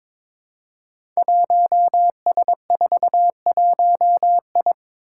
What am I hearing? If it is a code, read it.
Morse code, heard as a keyed tone, about 22 wpm, 700 Hz. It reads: 1S41I
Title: Five Challenging Letter-Number Combinations 22wpm